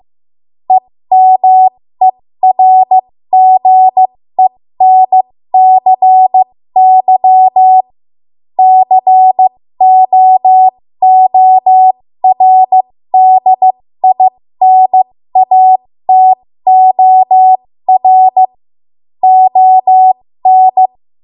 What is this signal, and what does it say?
Morse code, heard as a keyed tone, about 15 wpm, spaced as for 13 wpm, 750 Hz. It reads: EMERGENCY COORDINATOR ON
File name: Morse / 13WPM